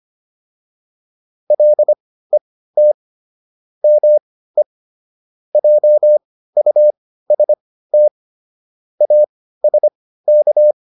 Morse code, heard as a keyed tone, about 25 wpm, spaced as for 15 wpm, 600 Hz. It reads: LET ME JUST ASK